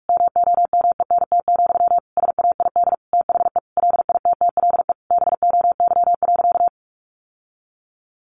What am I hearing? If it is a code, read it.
Morse code, heard as a keyed tone, about 44 wpm, 700 Hz. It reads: MOMENT, SAID THE LITTLE BOY.
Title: Sentences from Top 1000 Words 45wpm